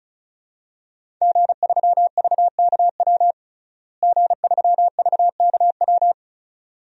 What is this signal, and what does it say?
Morse code, heard as a keyed tone, about 34 wpm, 700 Hz. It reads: G3VKW G3VKW
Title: DXCC-100 Call Signs - 5 Characters - Repeated 2x - 35wpm